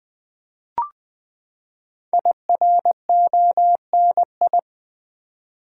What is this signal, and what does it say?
Morse code, heard as a keyed tone, about 20 wpm, 700 Hz. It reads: IRONI
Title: Build A Word - 5 Letters At A Time - 10-Letter Words - 20wpm